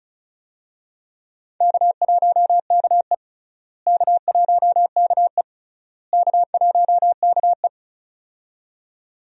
Morse code, heard as a keyed tone, about 34 wpm, 700 Hz. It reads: K1KE K1KE K1KE
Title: USA Call Signs - 4 Characters - Repeated 3x - 35wpm